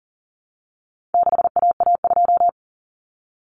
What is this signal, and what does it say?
Morse code, heard as a keyed tone, about 40 wpm, 700 Hz. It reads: 6AA2
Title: Common Cut Numbers - 4-Digit 40wpm